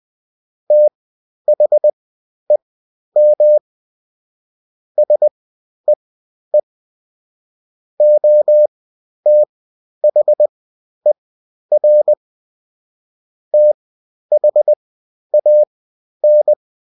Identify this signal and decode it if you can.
Morse code, heard as a keyed tone, about 20 wpm, spaced as for 11 wpm, 600 Hz. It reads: THEM SEE OTHER THAN
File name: Top-100-Words-10